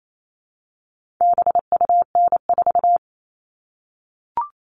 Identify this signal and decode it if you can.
Morse code, heard as a keyed tone, about 28 wpm, 700 Hz. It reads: BUN4
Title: All Cut Numbers - 4-Digit 28wpm